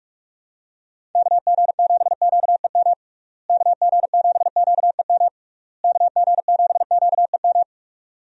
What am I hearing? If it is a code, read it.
Morse code, heard as a keyed tone, about 44 wpm, 700 Hz. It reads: KG7QEM KG7QEM KG7QEM
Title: USA Call Signs - 6 Characters - Repeated 3x - 45wpm